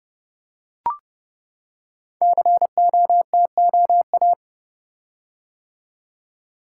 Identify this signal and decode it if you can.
Morse code, heard as a keyed tone, about 29 wpm, 700 Hz. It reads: COTOA
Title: Code Groups - 5 Letters - 30wpm